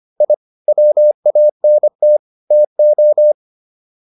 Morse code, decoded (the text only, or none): I WANT TO